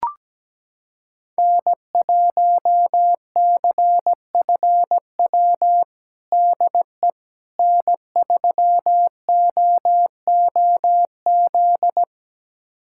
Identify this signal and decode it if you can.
Morse code, heard as a keyed tone, about 17 wpm, 700 Hz. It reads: N1CFW DE N3OOZ
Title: USA Callsign DE Callsign 17wpm